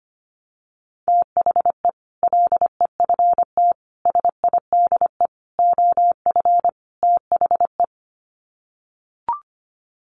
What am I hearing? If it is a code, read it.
Morse code, heard as a keyed tone, about 25 wpm, 700 Hz. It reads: THE LEFT SIDE OF THE
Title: Sets of 5 Words Encore 25wpm